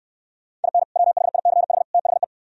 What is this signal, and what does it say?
Morse code, heard as a keyed tone, about 68 wpm, 700 Hz. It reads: IN WHICH THE